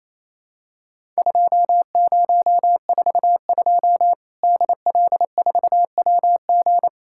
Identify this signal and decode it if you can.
Morse code, heard as a keyed tone, about 28 wpm, 700 Hz. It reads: 2042 DL4WG
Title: Worked All Europe DX Contest - QTC Exchange - 28wpm